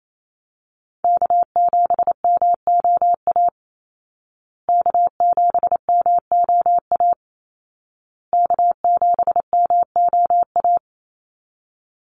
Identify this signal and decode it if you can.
Morse code, heard as a keyed tone, about 28 wpm, 700 Hz. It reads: K7MOA K7MOA K7MOA
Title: Call Signs Repeated 2x - 28wpm